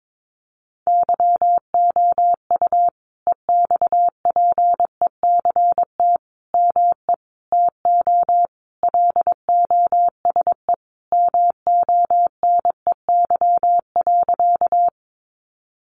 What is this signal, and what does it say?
Morse code, heard as a keyed tone, about 22 wpm, 700 Hz. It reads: YOU EXPECT ME TO LOSE MONEY.
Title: Sentences from Top 1100 Words 22wpm